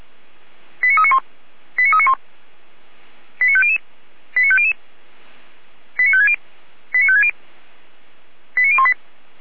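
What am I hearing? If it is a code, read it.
Five-tone selective call (ZVEI), heard as quick runs of five tones, tones of about 70 ms. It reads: 89391 89391 89500 89500 89560 89560 89018
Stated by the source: ZVEI five-tone call 89340 received by SDR.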